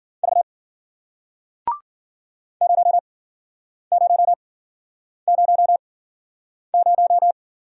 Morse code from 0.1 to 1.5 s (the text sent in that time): V